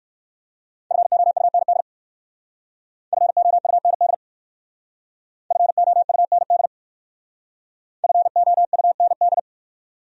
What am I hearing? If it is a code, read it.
Morse code, heard as a keyed tone, about 55 wpm, 700 Hz. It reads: FOUND FOUND FOUND FOUND
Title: Speed-Racing Top 300 Words 45wpm